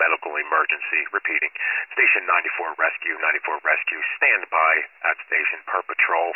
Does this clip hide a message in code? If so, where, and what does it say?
Yes: Emergency Alert System (SAME) tones (header bursts) from 1.1 to 5.3 s, heard as harsh buzzing bursts, with speech.